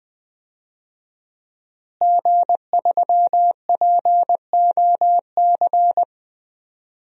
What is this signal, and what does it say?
Morse code, heard as a keyed tone, about 20 wpm, 700 Hz. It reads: G3POC
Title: DXCC-100 Call Signs - 5 Characters - Repeated 3x - 20wpm